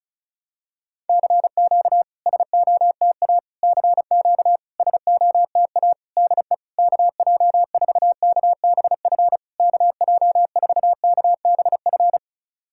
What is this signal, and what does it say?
Morse code, heard as a keyed tone, about 36 wpm, 700 Hz. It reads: CQ SOTA CQ SOTA DE KJ4KBF KJ4KBF
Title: SOTA QSO Practice 35wpm